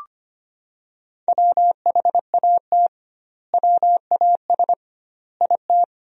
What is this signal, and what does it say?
Morse code, heard as a keyed tone, about 25 wpm, 700 Hz. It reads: WHAT WAS IT
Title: 2x Word-Spacing - Sentences from Top 300 Words - 25wpm